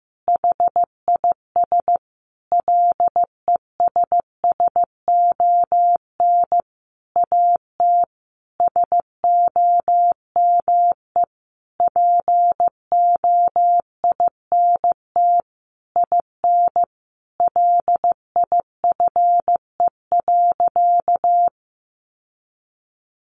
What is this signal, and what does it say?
Morse code, heard as a keyed tone, about 15 wpm, 700 Hz. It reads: HIS LESSON AT SOME POINT IN LIFE.
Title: Sentences from Top 1200 Words 15wpm